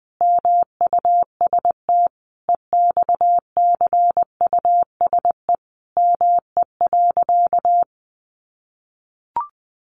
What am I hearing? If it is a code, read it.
Morse code, heard as a keyed tone, about 20 wpm, 700 Hz. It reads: MUST EXCUSE ME.
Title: Sentences from Top 700 Words 20wpm